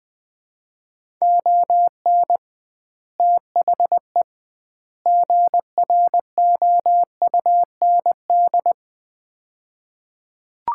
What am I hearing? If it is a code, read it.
Morse code, heard as a keyed tone, about 21 wpm, 700 Hz. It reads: ON THE GROUND
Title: 2x Word-Spacing - Sets of 3 Words - 20wpm